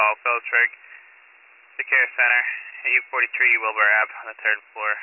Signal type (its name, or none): Emergency Alert System (SAME) tones